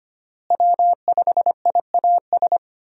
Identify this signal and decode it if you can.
Morse code, heard as a keyed tone, about 25 wpm, 700 Hz. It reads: W5IAS
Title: USA Call Signs - 5 Characters - Repeated 3x - 25wpm